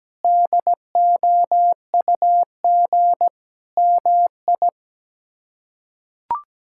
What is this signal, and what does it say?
Morse code, heard as a keyed tone, about 17 wpm, 700 Hz. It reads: DOUG MI
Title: K1USN SST QSO Practice - 17wpm